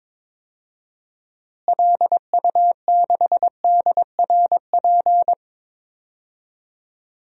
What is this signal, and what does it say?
Morse code, heard as a keyed tone, about 21 wpm, 700 Hz. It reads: LU6DRP
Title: DXCC-100 Call Signs - 6 Characters - Repeated 3x - 22wpm